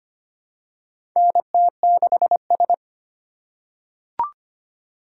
Morse code, heard as a keyed tone, about 25 wpm, 700 Hz. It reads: NT6S
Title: USA Call Signs - 4 Characters - 4x Character Spacing - 25wpm